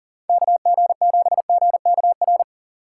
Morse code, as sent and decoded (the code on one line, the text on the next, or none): -.- --. --... --. -.- .-.
KG7GKR